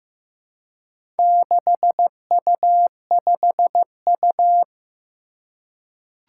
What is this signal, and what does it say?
Morse code, heard as a keyed tone, about 15 wpm, 700 Hz. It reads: 6U5U
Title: Four Challenging Letter-Number Combinations 15wpm